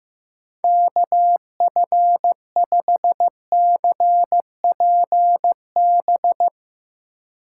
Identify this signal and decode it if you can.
Morse code, heard as a keyed tone, about 15 wpm, 700 Hz. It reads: KF5CPB
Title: USA Call Signs - 6 Characters - 2x Character Spacing - 15wpm